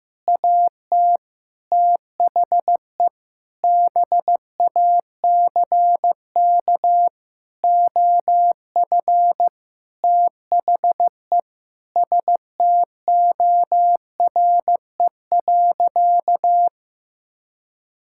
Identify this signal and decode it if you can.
Morse code, heard as a keyed tone, about 15 wpm, 700 Hz. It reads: AT THE BACK OF THE STORE.
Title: Sentences from Top 1000 Words - Review Part 5 - 15wpm